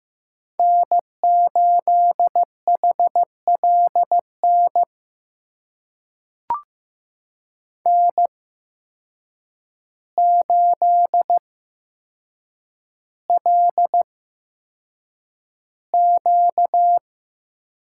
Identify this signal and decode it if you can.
Morse code, heard as a keyed tone, about 15 wpm, spaced as for 7 wpm, 700 Hz. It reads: N8HLN N8LQ
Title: USA Call Signs - 5 Characters - 8x Character Spacing - 15wpm